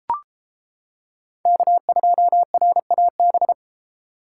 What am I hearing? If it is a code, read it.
Morse code, heard as a keyed tone, about 31 wpm, 700 Hz. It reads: K2RAB